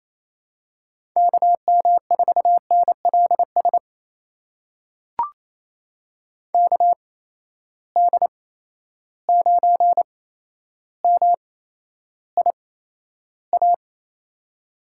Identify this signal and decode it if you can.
Morse code, heard as a keyed tone, about 28 wpm, spaced as for 12 wpm, 700 Hz. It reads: KM4NLS KD9MIA
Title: USA Call Signs - 6 Characters - 8x Character Spacing - 28wpm